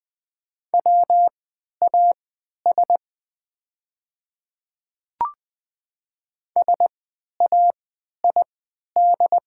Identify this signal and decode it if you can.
Morse code, heard as a keyed tone, about 20 wpm, spaced as for 11 wpm, 690 Hz. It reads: WAS SAID
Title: 3x Character Spacing - Top 500 Words - 20wpm